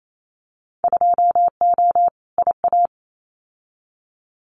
Morse code, heard as a keyed tone, about 28 wpm, 700 Hz. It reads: 2O IA
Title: Winter Field Day Exchanges 28wpm